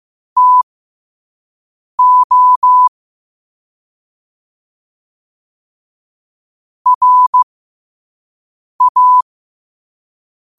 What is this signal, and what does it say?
Morse code, heard as a keyed tone, about 15 wpm, spaced as for 5 wpm, 1000 Hz. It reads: TO RA
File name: qotdfiles / 5wpm